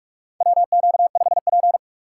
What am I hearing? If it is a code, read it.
Morse code, heard as a keyed tone, about 47 wpm, 700 Hz. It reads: WQ5P
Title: USA Call Signs - 6x Character Spacing - 45wpm